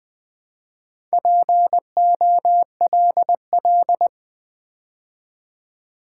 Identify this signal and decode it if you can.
Morse code, heard as a keyed tone, about 20 wpm, 690 Hz. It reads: POLL
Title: Build A Word - 2 Letters At A Time - 4-Letter Words - 20wpm